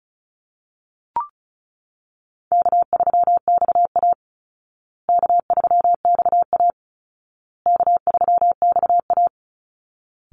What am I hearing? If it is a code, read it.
Morse code, heard as a keyed tone, about 34 wpm, 700 Hz. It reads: K3XA K3XA K3XA
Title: Easy Call Signs Repeated 3x - 35wpm